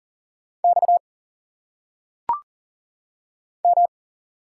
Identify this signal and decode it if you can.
Morse code, heard as a keyed tone, about 40 wpm, 700 Hz. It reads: X M
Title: Single Letters - Phonetic Pronunciation 40wpm